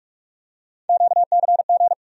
Morse code, heard as a keyed tone, about 44 wpm, 700 Hz. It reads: QCG
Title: USA Call Sign Postfixes - 3 Letters 45wpm